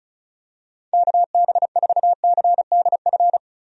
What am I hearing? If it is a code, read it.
Morse code, heard as a keyed tone, about 34 wpm, 700 Hz. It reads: KB4CDF
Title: USA Call Signs - 6 Characters - Repeated 2x - 35wpm